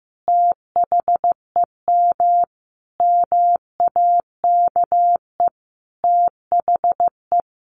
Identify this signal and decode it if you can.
Morse code, heard as a keyed tone, about 15 wpm, 700 Hz. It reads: THEM MAKE THE
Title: Sentences from Top 1300 Words 15wpm